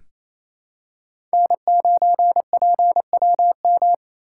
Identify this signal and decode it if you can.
Morse code, heard as a keyed tone, about 28 wpm, 700 Hz. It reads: N9PWM
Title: USA Call Signs - 5 Characters - 2x Character Spacing - 28wpm